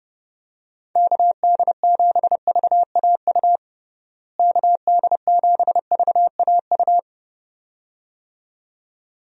Morse code, heard as a keyed tone, about 30 wpm, 700 Hz. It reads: KD7VAU KD7VAU
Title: USA Call Signs - 6 Characters - Repeated 2x - 30wpm